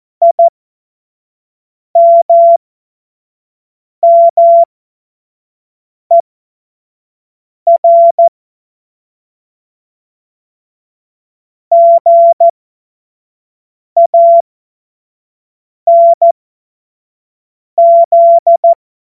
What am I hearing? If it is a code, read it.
Morse code, heard as a keyed tone, about 14 wpm, spaced as for 5 wpm, 680 Hz. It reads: IMMER GANZ
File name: Morsekurs / Witze / Witz 19 März 2024 25 F70